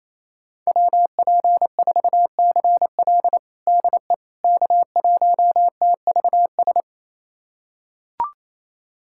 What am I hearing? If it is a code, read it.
Morse code, heard as a keyed tone, about 27 wpm, 700 Hz. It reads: WP4CL DE K1TVS
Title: USA Callsign DE Callsign - 5 Characters - 28wpm